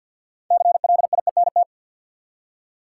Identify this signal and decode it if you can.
Morse code, heard as a keyed tone, about 50 wpm, 700 Hz. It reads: CLIENT
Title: Six Letter Words - 50wpm